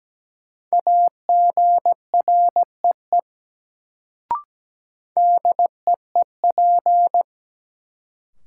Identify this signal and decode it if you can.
Morse code, heard as a keyed tone, about 17 wpm, 700 Hz. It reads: AGREE DEEP
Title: Sets of 5 Words Encore - New Word Drill - 17wpm